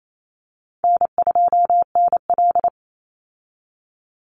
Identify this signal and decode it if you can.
Morse code, heard as a keyed tone, about 28 wpm, 700 Hz. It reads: N2NL
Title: Call Signs Repeated 3x - 28wpm